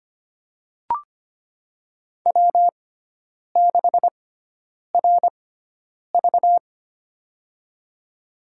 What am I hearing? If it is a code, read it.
Morse code, heard as a keyed tone, about 25 wpm, spaced as for 9 wpm, 700 Hz. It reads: W6RV
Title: USA Call Signs - 4 Characters - 6x Character Spacing - 25wpm